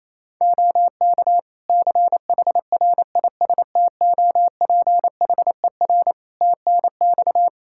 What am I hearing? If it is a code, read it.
Morse code, heard as a keyed tone, about 28 wpm, 700 Hz. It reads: OK CHRISTOPHER TNX
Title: Practice QSO - Exchange 4 - Ending - 28wpm